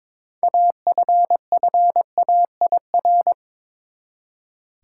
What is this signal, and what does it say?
Morse code, heard as a keyed tone, about 22 wpm, 700 Hz. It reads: AFFAIR
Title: Six Letter Words - 22wpm